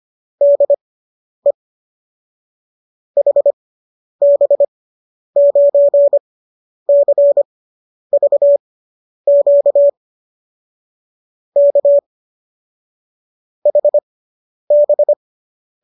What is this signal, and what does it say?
Morse code, heard as a keyed tone, about 25 wpm, spaced as for 10 wpm, 570 Hz. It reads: DE HB9CVQ K HB